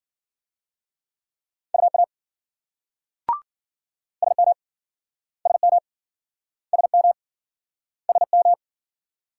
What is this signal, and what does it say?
Morse code, heard as a keyed tone, about 53 wpm, 700 Hz. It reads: UA SM SM SM SM